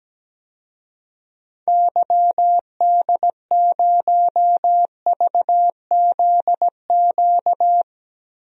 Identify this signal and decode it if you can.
Morse code, heard as a keyed tone, about 17 wpm, 700 Hz. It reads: YD0VZQ
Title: DXCC-100 Call Signs - 6 Characters - 17wpm